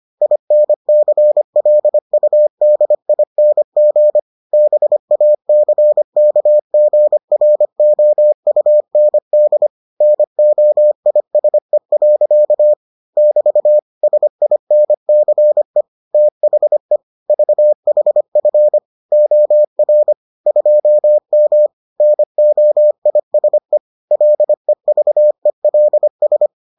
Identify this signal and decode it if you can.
Morse code, heard as a keyed tone, about 25 wpm, 600 Hz. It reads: INCLUDING BACKGROUND NOISE. <BT> SINCE THE VHF OR 2M NOISE LEVELS